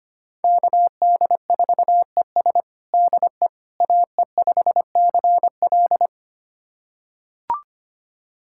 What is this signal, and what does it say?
Morse code, heard as a keyed tone, about 25 wpm, 700 Hz. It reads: KD4ES DE AE5CL
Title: USA Callsign DE Callsign - 5 Characters - 25wpm